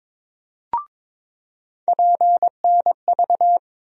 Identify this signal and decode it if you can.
Morse code, heard as a keyed tone, about 23 wpm, 700 Hz. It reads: PNV
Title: USA Call Sign Postfixes - 3 Letters 22wpm